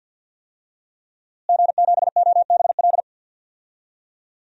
Morse code, heard as a keyed tone, about 52 wpm, 700 Hz. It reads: G7OBL
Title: DXCC-100 Call Signs - 5 Characters - Repeated 3x - 50wpm